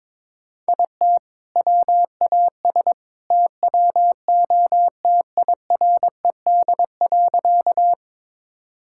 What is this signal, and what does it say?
Morse code, heard as a keyed tone, about 22 wpm, 700 Hz. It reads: IT WAS TWO TIRED.
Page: morse-code-ninja-practice-22wpm